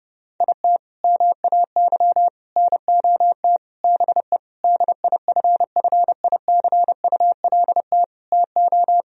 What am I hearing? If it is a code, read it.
Morse code, heard as a keyed tone, about 30 wpm, 700 Hz. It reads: IT MAY NOT BE DIFFICULT TO
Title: Sentences from Top 1000 Words - Review Part 4 - 30wpm